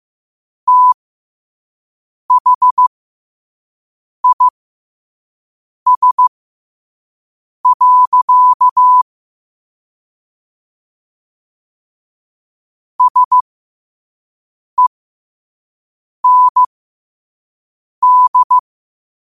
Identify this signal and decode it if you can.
Morse code, heard as a keyed tone, about 15 wpm, spaced as for 5 wpm, 1000 Hz. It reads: THIS. SEND
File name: qotdfiles / 5wpm